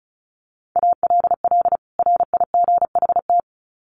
Morse code, heard as a keyed tone, about 34 wpm, 700 Hz. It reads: ALL RIGHT
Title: Sets of 2 Words 35wpm